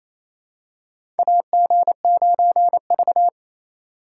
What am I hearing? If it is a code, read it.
Morse code, heard as a keyed tone, about 28 wpm, 700 Hz. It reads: AG9V
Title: USA Call Signs - 4 Characters - 3x Character Spacing - 28wpm